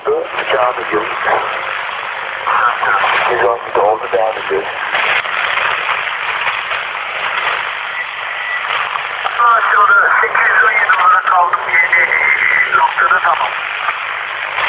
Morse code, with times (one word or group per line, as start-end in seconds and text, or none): none